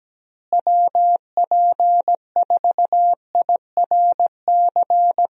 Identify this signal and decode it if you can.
Morse code, heard as a keyed tone, about 17 wpm, 700 Hz. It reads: WP4IRC
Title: USA Call Signs - 6 Characters - Repeated 3x - 17wpm